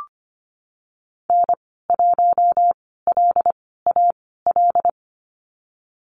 Morse code, heard as a keyed tone, about 25 wpm, spaced as for 16 wpm, 700 Hz. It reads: N1LAL
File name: Lesson 15g - Number 5 - Call Signs 25wpm